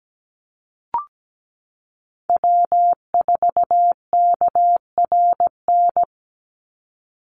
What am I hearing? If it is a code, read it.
Morse code, heard as a keyed tone, about 17 wpm, 700 Hz. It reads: W4KRN
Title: NA SOTA Chaser Callsigns 17wpm